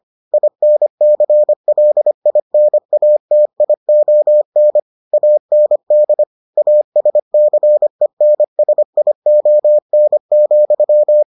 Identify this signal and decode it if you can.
Morse code, heard as a keyed tone, about 25 wpm, 600 Hz. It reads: INCLINATION AND ASCENSION,